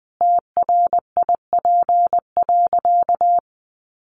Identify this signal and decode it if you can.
Morse code, heard as a keyed tone, about 20 wpm, 700 Hz. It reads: TRIP.
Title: Sentences from Top 1300 Words 20wpm